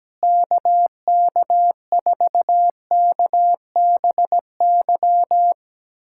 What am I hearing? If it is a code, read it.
Morse code, heard as a keyed tone, about 17 wpm, 700 Hz. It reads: KK4KBY
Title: USA Callsign DE Callsign - 6 Characters - 17wpm